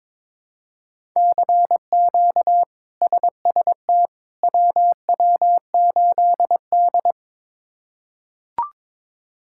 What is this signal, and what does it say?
Morse code, heard as a keyed tone, about 22 wpm, 700 Hz. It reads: CQ SST WW8D
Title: K1USN SST QSO Practice - 22wpm